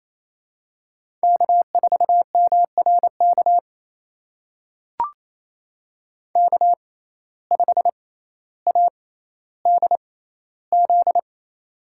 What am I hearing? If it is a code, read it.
Morse code, heard as a keyed tone, about 27 wpm, spaced as for 14 wpm, 700 Hz. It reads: K4MRK K5ADZ
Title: USA Call Signs - 5 Characters - 6x Character Spacing - 28wpm